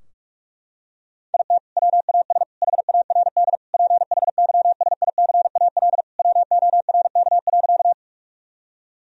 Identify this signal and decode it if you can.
Morse code, heard as a keyed tone, about 46 wpm, 700 Hz. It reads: IT WAS HARD PHYSICAL WORK.
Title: Sentences from Top 1000 Words - Review Part 4 - 45wpm